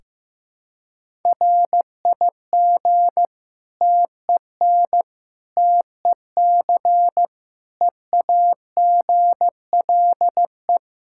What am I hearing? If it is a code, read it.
Morse code, heard as a keyed tone, about 15 wpm, 700 Hz. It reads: RIG TEN TEC EAGLE
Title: Practice QSO - Exchange 2 - Info - 15wpm